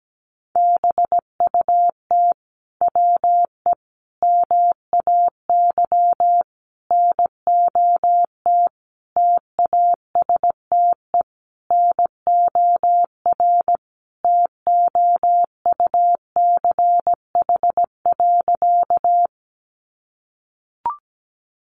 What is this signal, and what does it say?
Morse code, heard as a keyed tone, about 17 wpm, 700 Hz. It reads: BUT WE MAY NOT TASTE NOR TOUCH.
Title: Sentences from Top 1500 Words 17wpm